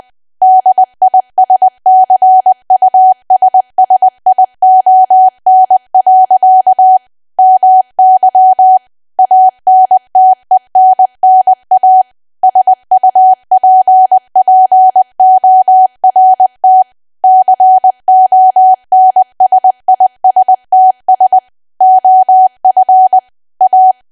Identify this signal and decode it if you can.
Morse code, heard as a keyed tone, about 20 wpm, 750 Hz. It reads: DISCUSSION. MY ANTENNA SUPPORT CONSISTS OF A
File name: cwa / W1AW-Level3 / 20WPM